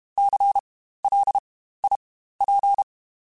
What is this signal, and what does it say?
Morse code, heard as a keyed tone, about 32 wpm, spaced as for 15 wpm, 800 Hz. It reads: CLIP
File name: Spaced Alphas / Spaced Alphas mp3s / Cs sp @30